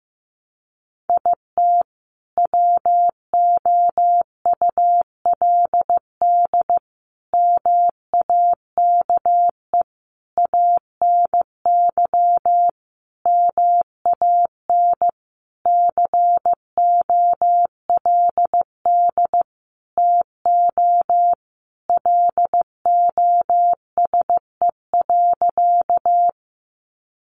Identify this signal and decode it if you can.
Morse code, heard as a keyed tone, about 15 wpm, 700 Hz. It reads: IT WOULD MAKE ANY MAN COLD TO LOSE.
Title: Sentences from Top 1100 Words 15wpm